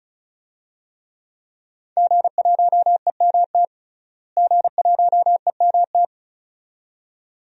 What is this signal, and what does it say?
Morse code, heard as a keyed tone, about 36 wpm, 700 Hz. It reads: G1EMT G1EMT
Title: DXCC-100 Call Signs - 5 Characters - Repeated 2x - 35wpm